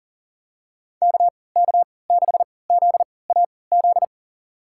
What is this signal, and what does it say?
Morse code, heard as a keyed tone, about 40 wpm, spaced as for 23 wpm, 700 Hz. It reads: KK6ZAZ
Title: USA Call Signs - 6 Characters - 3x Character Spacing - 40wpm